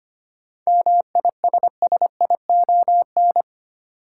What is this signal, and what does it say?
Morse code, heard as a keyed tone, about 25 wpm, 700 Hz. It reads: MISSION
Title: Sentences from Top 1600 Words - New Word Drill Alt - 25wpm